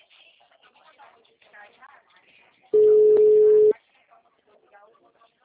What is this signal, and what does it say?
Ringback tone (Europe, 425 Hz), with speech.